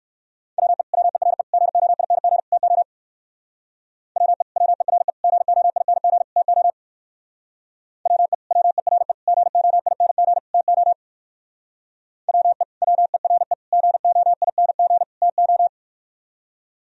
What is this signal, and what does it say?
Morse code, heard as a keyed tone, about 66 wpm, 700 Hz. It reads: WE WERE GOING TO WE WERE GOING TO WE WERE GOING TO WE WERE GOING TO